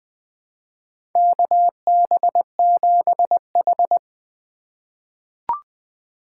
Morse code, heard as a keyed tone, about 20 wpm, 700 Hz. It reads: KB7H
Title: K1USN SST QSO Practice - 20wpm